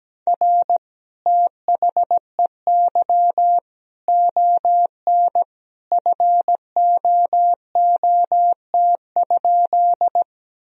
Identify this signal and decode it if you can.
Morse code, heard as a keyed tone, about 17 wpm, 700 Hz. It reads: R THEY ON FOOT?
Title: Sentences from Top 500 Words and Abbreviations 17wpm